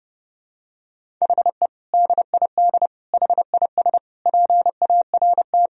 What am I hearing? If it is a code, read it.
Morse code, heard as a keyed tone, about 30 wpm, 700 Hz. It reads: HE DID HIS PART
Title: Sentences from Top 1000 Words - Review Part 1 - 30wpm